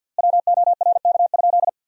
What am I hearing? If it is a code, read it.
Morse code, heard as a keyed tone, about 53 wpm, 700 Hz. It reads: WORK?
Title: 4x Word-Spacing - Sentences from Top 500 Words - 50wpm